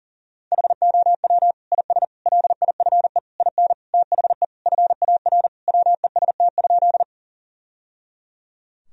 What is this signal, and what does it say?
Morse code, heard as a keyed tone, about 37 wpm, 700 Hz. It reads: HOW IS LIFE IN THE FAR WEST?
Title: Sentences from Top 1000 Words - Review Part 2 - 40wpm